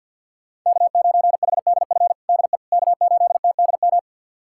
Morse code, heard as a keyed tone, about 52 wpm, 690 Hz. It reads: K9HDF DE K8TDM
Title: USA Callsign DE Callsign - 5 Characters - 50wpm